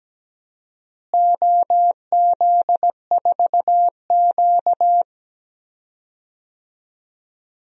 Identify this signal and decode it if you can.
Morse code, heard as a keyed tone, about 17 wpm, 700 Hz. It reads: OZ4Q
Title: DXCC-100 Call Signs - 4 Characters - Repeated 3x - 17wpm